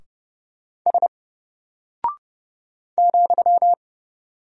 Morse code, heard as a keyed tone, about 31 wpm, 700 Hz. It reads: S ,
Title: Single Character (Letter:Number:Punctuation:Prosign) 30wpm